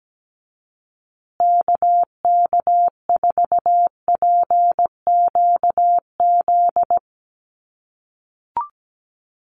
Morse code, heard as a keyed tone, about 17 wpm, 700 Hz. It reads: KK4PQZ
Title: USA Call Signs Repeated 2x 17wpm